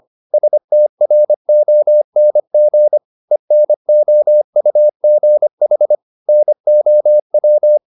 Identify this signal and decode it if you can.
Morse code, heard as a keyed tone, about 25 wpm, 600 Hz. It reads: STRONG ENOUGH NOW
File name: the old man and the sea part 1 25wpm